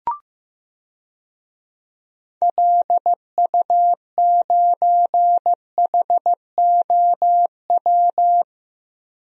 Morse code, heard as a keyed tone, about 15 wpm, 700 Hz. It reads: LU9HOW
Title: DXCC-100 Call Signs - 6 Characters - 15wpm